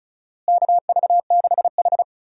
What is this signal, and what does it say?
Morse code, heard as a keyed tone, about 36 wpm, 700 Hz. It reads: KV6H
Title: USA Call Signs - 4 Characters - Repeated 3x - 35wpm